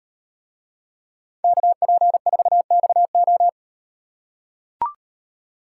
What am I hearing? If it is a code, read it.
Morse code, heard as a keyed tone, about 38 wpm, 700 Hz. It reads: KP4XO